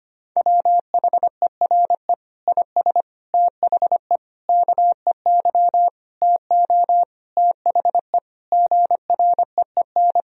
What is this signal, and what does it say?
Morse code, heard as a keyed tone, about 25 wpm, 700 Hz. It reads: WHERE IS THE KEY TO THE GREEN
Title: Sentences from Top 1000 Words - Review Part 4 - 25wpm